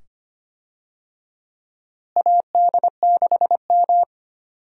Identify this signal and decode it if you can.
Morse code, heard as a keyed tone, about 25 wpm, 700 Hz. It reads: AD6M
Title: USA Call Signs - 4 Characters - Repeated 2x - 25wpm